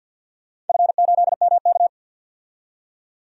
Morse code, heard as a keyed tone, about 49 wpm, 700 Hz. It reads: F8MK
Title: DXCC-100 Call Signs - 4 Characters - 50wpm